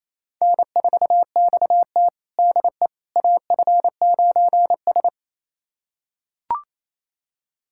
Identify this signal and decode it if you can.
Morse code, heard as a keyed tone, about 28 wpm, 700 Hz. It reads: N4XT DE AF9S
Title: USA Callsign DE Callsign - 4 Characters - 28wpm